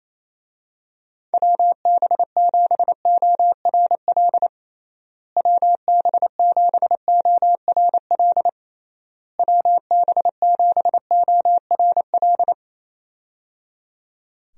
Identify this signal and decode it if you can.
Morse code, heard as a keyed tone, about 28 wpm, 700 Hz. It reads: WB7ORL WB7ORL WB7ORL
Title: USA Call Signs - 6 Characters - Repeated 3x - 28wpm